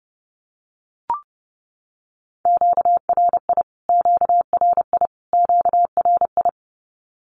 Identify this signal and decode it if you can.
Morse code, heard as a keyed tone, about 30 wpm, 700 Hz. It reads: QRI QRI QRI
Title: Q-Codes Repeated 3x 30wpm